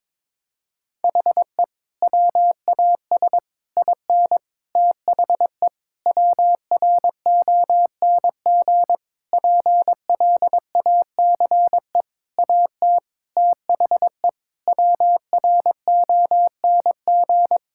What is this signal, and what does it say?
Morse code, heard as a keyed tone, about 23 wpm, 700 Hz. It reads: HE WAS IN THE WRONG PLACE AT THE WRONG
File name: practice / 22wpm / Sentences with 8-Word Phrases - 22wpm